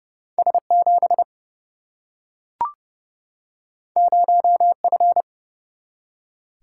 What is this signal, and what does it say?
Morse code, heard as a keyed tone, about 30 wpm, 700 Hz. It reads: S7 0F
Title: Single-Letter Single-Number 30wpm